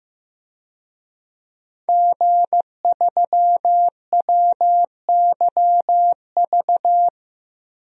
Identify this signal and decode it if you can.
Morse code, heard as a keyed tone, about 15 wpm, 700 Hz. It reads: G3WYV
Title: DXCC-100 Call Signs - 5 Characters - Repeated 3x - 15wpm